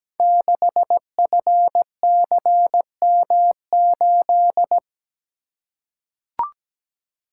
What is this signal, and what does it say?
Morse code, heard as a keyed tone, about 17 wpm, 700 Hz. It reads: 6FCM8
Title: Code Groups - 5 Letter:Numbers - 17wpm